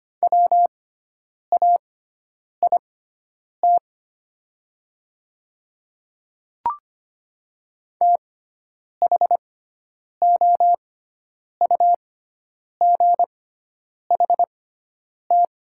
Morse code, heard as a keyed tone, about 25 wpm, spaced as for 9 wpm, 700 Hz. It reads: WAIT THOUGHT
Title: 6x Character Spacing - Top 500 Words - 25wpm